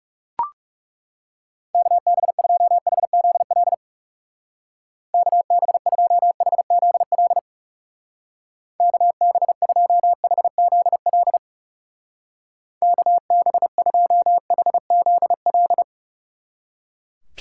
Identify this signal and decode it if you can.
Morse code, heard as a keyed tone, about 36 wpm, 700 Hz. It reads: KB2HZL KB2HZL KB2HZL KB2HZL
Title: Speed Racing - USA Call Signs 30wpm